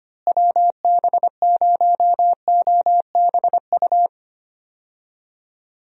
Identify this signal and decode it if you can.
Morse code, heard as a keyed tone, about 25 wpm, 700 Hz. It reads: WB0OBU
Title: USA Call Signs - 6 Characters - Repeated 3x - 25wpm